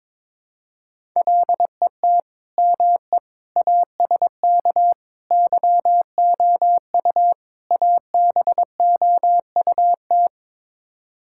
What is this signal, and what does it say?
Morse code, heard as a keyed tone, about 22 wpm, 700 Hz. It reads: LET ME ASK YOU ABOUT
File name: practice / 22wpm / Build A Phrase - 5-Word Phrases - 22wpm